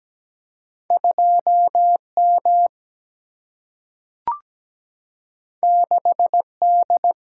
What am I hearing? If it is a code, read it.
Morse code, heard as a keyed tone, about 17 wpm, 700 Hz. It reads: 2M 6D
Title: Single-Letter Single-Number 17wpm